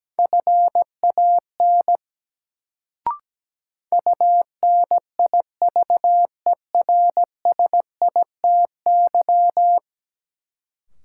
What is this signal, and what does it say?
Morse code, heard as a keyed tone, about 17 wpm, 700 Hz. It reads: FAN UNIVERSITY
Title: Sentences from Top 1300 Words - New Word Drill Alt - 17wpm